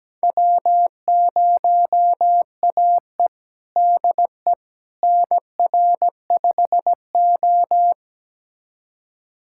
Morse code, heard as a keyed tone, about 17 wpm, 700 Hz. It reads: W0AE DE NR5O
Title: USA Callsign DE Callsign - 4 Characters - 17wpm